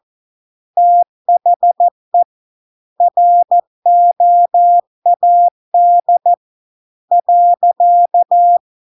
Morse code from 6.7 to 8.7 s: .